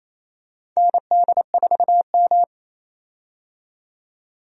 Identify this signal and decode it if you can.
Morse code, heard as a keyed tone, about 28 wpm, 700 Hz. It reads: ND4M
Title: USA Call Signs - 4 Characters - 28wpm